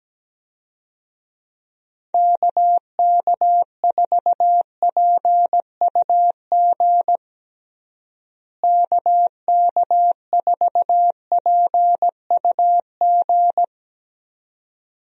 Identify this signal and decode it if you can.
Morse code, heard as a keyed tone, about 17 wpm, 700 Hz. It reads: KK4PUG KK4PUG
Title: USA Call Signs - 6 Characters - Repeated 2x - 17wpm